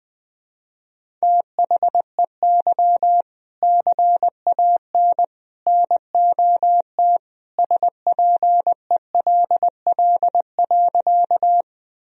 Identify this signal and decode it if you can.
Morse code, heard as a keyed tone, about 20 wpm, 700 Hz. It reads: THEY CAN NOT SPELL.
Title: Sentences from Top 1000 Words - Review Part 1 - 20wpm